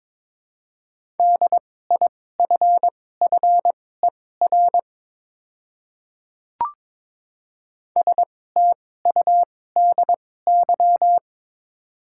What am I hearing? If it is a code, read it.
Morse code, heard as a keyed tone, about 22 wpm, spaced as for 16 wpm, 700 Hz. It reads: DIFFER STUDY
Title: 2x Character Spacing - Top 300 Words - 22wpm